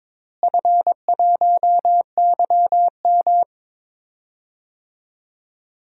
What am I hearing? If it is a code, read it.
Morse code, heard as a keyed tone, about 22 wpm, 700 Hz. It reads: F1YM
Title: DXCC-100 Call Signs - 4 Characters - Repeated 2x - 22wpm